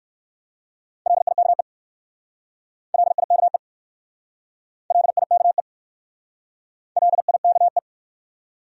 Morse code, heard as a keyed tone, about 55 wpm, 690 Hz. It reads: LIKE LIKE LIKE LIKE
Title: Speed-Racing Top 500 Words 45wpm